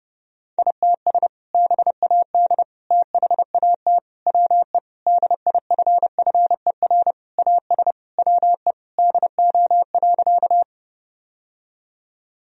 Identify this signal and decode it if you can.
Morse code, heard as a keyed tone, about 32 wpm, 700 Hz. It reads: ITS BAD THAT WE DIFFER AS WE DO.
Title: Sentences from Top 1000 Words - Review Part 1 - 30wpm